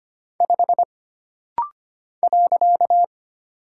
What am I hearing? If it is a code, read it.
Morse code, heard as a keyed tone, about 25 wpm, 700 Hz. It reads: 5 .
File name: Single Character (Letter:Number:Punctuation:Prosign) - Rapid-Fire - 25wpm